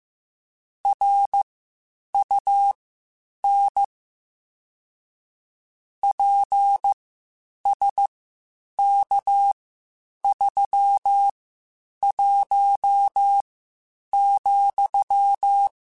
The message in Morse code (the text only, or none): RUN PSK31,